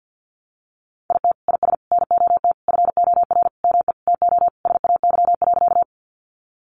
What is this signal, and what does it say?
Morse code, heard as a keyed tone, about 50 wpm, 700 Hz. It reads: IT IS NOT FOR ME TO SAY.
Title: Sentences from Top 600 Words 50wpm